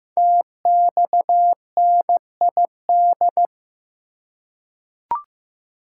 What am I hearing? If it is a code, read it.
Morse code, heard as a keyed tone, about 15 wpm, 700 Hz. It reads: TXNID
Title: Code Groups - 5 Letter:Numbers - 15wpm